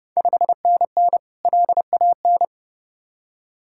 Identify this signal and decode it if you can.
Morse code, heard as a keyed tone, about 32 wpm, 700 Hz. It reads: 5NN LAN
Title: Ontario QSO Party Exchange 30wpm